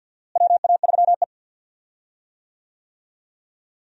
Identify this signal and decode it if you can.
Morse code, heard as a keyed tone, about 51 wpm, 700 Hz. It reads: WA3E
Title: USA Call Signs - 4 Characters - Repeated 2x - 50wpm